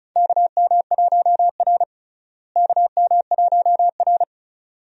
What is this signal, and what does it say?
Morse code, heard as a keyed tone, about 36 wpm, 700 Hz. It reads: KM1R KM1R
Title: USA Call Signs - 4 Characters - Repeated 2x - 35wpm